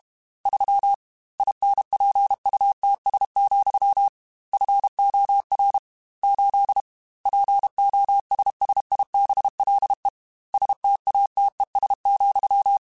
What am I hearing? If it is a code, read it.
Morse code, heard as a keyed tone, about 32 wpm, 800 Hz. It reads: 3 INPUTS, FOR 8 POSSIBLE STATES,
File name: Technical Subjects / Technical Subjects-mp3s / AND Gate @30